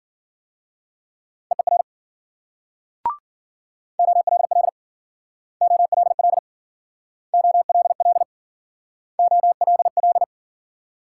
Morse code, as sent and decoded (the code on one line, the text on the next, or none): . .-. / --- .-.. .-.. / --- .-.. .-.. / --- .-.. .-.. / --- .-.. .-..
EER OLL OLL OLL OLL